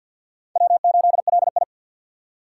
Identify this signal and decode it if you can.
Morse code, heard as a keyed tone, about 51 wpm, 700 Hz. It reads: W8LI
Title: USA Call Signs - 3x Character Spacing - 50wpm